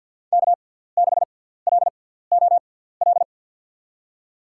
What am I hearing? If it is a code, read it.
Morse code, heard as a keyed tone, about 53 wpm, spaced as for 17 wpm, 700 Hz. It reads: K6LOL
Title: USA Call Signs - 5 Characters - 6x Character Spacing - 50wpm